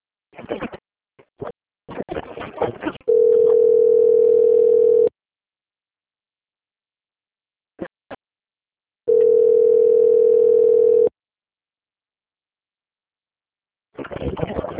Ringback tone (North America), 440 + 480 Hz.